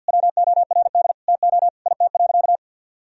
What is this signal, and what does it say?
Morse code, heard as a keyed tone, about 45 wpm, 700 Hz. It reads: WORD TO IT.
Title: Sentences from Top 1000 Words - Review Part 1 - 50wpm